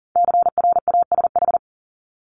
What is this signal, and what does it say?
Morse code, heard as a keyed tone, about 38 wpm, 700 Hz. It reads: CRASH
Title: Multi-letter Phonemes Repeated 3x 40wpm